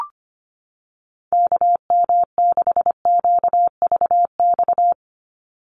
Morse code, heard as a keyed tone, about 25 wpm, 700 Hz. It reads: KM6QVX
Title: USA Call Signs 25wpm